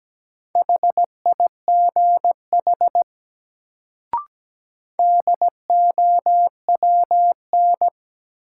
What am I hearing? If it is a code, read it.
Morse code, heard as a keyed tone, about 17 wpm, 700 Hz. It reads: HIGH DOWN
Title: Sets of 2 Words - New Word Drill - 17wpm